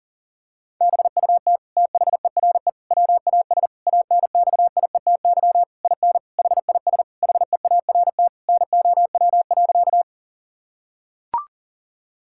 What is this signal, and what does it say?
Morse code, heard as a keyed tone, about 40 wpm, 700 Hz. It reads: BUT THERE WAS ANXIETY IN HIS HEART NOW.
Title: Sentences from Top 1600 Words 40wpm